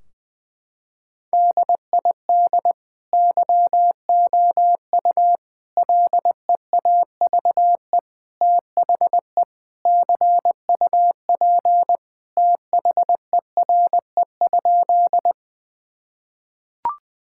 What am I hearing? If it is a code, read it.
Morse code, heard as a keyed tone, about 20 wpm, 700 Hz. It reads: DID YOU LEAVE THE CUP THERE?
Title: Sentences from Top 1600 Words 20wpm